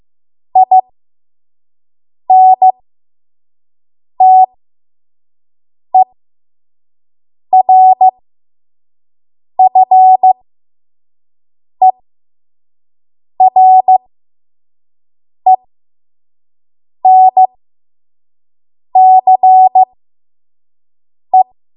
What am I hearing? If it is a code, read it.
Morse code, heard as a keyed tone, about 15 wpm, spaced as for 5 wpm, 750 Hz. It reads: INTERFERENCE
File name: Morse / 05WPM